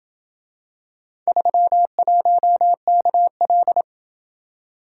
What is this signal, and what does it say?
Morse code, heard as a keyed tone, about 27 wpm, 700 Hz. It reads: 31KL